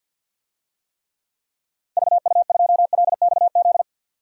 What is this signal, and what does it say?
Morse code, heard as a keyed tone, about 50 wpm, 700 Hz. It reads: VU2LXZ
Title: DXCC-100 Call Signs - 6 Characters - 50wpm